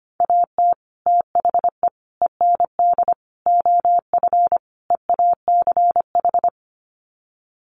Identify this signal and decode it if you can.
Morse code, heard as a keyed tone, about 25 wpm, 700 Hz. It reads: AT THE END OF EACH
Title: Sets of 5 Words Encore 25wpm